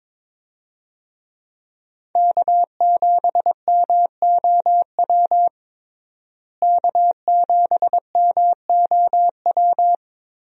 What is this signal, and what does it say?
Morse code, heard as a keyed tone, about 22 wpm, 700 Hz. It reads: K7MOW K7MOW
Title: USA Call Signs - 5 Characters - Repeated 2x - 22wpm